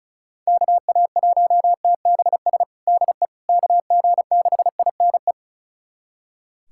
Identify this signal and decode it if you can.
Morse code, heard as a keyed tone, about 34 wpm, 700 Hz. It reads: KA1TBS DE KG6INE
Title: USA Callsign DE Callsign 35wpm